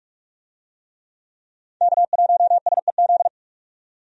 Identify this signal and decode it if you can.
Morse code, heard as a keyed tone, about 44 wpm, 700 Hz. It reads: K1SEZ
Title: USA Call Signs - 5 Characters - 45wpm